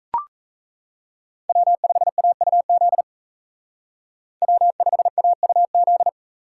Morse code, heard as a keyed tone, about 36 wpm, 700 Hz. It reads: W5AUZ W5AUZ